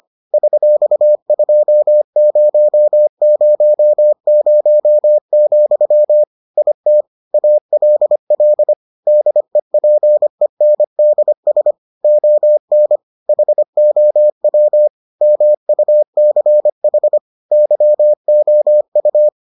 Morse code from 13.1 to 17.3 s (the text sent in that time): HOW MUCH